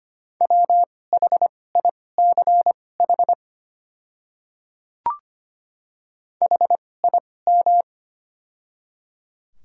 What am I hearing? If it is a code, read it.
Morse code, heard as a keyed tone, about 25 wpm, spaced as for 18 wpm, 690 Hz. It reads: WHICH HIM
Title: 2x Character Spacing - Top 100 Words - 25wpm